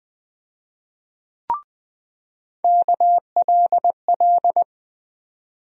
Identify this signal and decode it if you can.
Morse code, heard as a keyed tone, about 20 wpm, 700 Hz. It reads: KLL
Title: USA Call Sign Postfixes - 3 Letters 20wpm